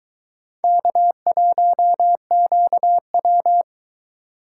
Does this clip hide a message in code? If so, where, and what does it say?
Yes: Morse code from 0.6 to 3.6 s, heard as a keyed tone, about 22 wpm, 700 Hz. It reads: K1QW